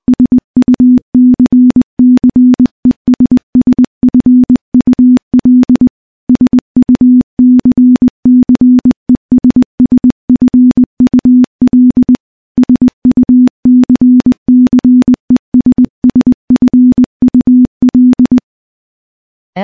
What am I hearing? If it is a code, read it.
Morse code, heard as a keyed tone, about 20 wpm, 250 Hz. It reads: SUCCESSFUL SUCCESSFUL SUCCESSFUL
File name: morse / morse-training-20WPM